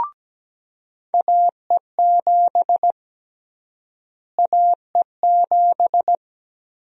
Morse code, heard as a keyed tone, about 17 wpm, 700 Hz. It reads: AE7 AE7
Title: USA Call Sign Prefixes - Repeated 2x - 17wpm